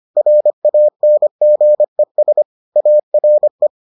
Morse code, heard as a keyed tone, about 25 wpm, 600 Hz. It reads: RANGES ARE